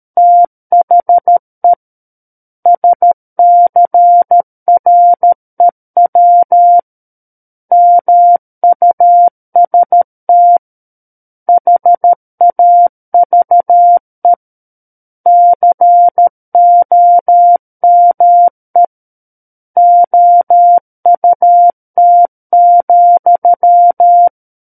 Morse code, heard as a keyed tone, about 13 wpm, 700 Hz. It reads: THE SCREW MUST HAVE COME OUT,
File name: Book One / 04-Chapter 4 - 13 WPM